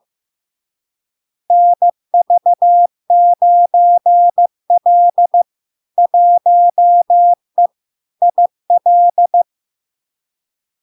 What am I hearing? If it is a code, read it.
Morse code, heard as a keyed tone, about 15 wpm, 700 Hz. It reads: NV9L 1E IL
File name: LARK-FD-2022-Log-in-CW-15wpm-700-Hz